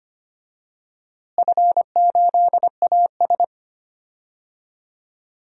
Morse code, heard as a keyed tone, about 25 wpm, 700 Hz. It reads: F8AS
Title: DXCC-100 Call Signs - 4 Characters - Repeated 3x - 25wpm